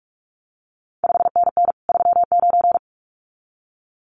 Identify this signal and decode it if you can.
Morse code, heard as a keyed tone, about 46 wpm, 700 Hz. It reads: 5NN 39
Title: CQ WW DX Contest Exchange Encore - 45wpm